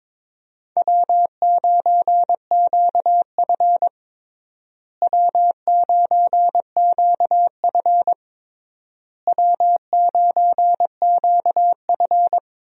Morse code, heard as a keyed tone, about 22 wpm, 700 Hz. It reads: W9QF W9QF W9QF
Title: USA Call Signs - 4 Characters - Repeated 3x - 22wpm